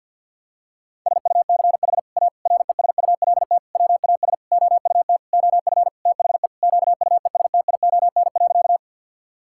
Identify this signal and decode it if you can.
Morse code, heard as a keyed tone, about 50 wpm, 700 Hz. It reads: SUCH A RESULT WAS OUT OF THE QUESTION.
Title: Sentences from Top 1000 Words - Review Part 5 - 50wpm